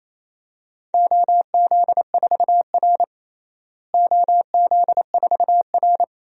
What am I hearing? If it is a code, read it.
Morse code, heard as a keyed tone, about 28 wpm, 700 Hz. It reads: OZ4R OZ4R
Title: DXCC-100 Call Signs - 4 Characters - Repeated 2x - 28wpm